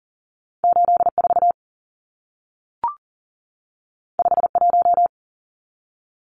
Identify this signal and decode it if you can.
Morse code, heard as a keyed tone, about 40 wpm, 700 Hz. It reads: 84 51
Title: Numbers - 2-Digit 40wpm